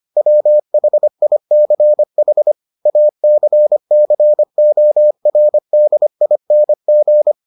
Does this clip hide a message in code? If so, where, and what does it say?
Yes: Morse code from 0.2 to 7.3 s, heard as a keyed tone, about 25 wpm, 600 Hz. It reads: WHICH ACCORDING